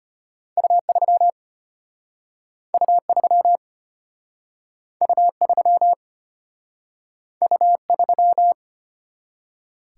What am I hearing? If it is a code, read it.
Morse code, heard as a keyed tone, about 35 wpm, 700 Hz. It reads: U3 U3 U3 U3